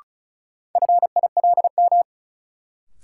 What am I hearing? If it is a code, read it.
Morse code, heard as a keyed tone, about 36 wpm, 700 Hz. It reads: FILM
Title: Sentences from Top 1000 Words - Review Part 3 - New Word Drill - 35wpm